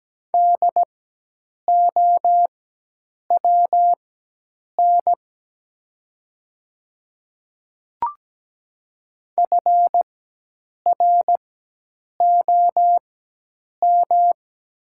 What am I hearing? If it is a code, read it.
Morse code, heard as a keyed tone, about 17 wpm, spaced as for 8 wpm, 700 Hz. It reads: DOWN FROM
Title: 4x Character Spacing - Top 100 Words - 17wpm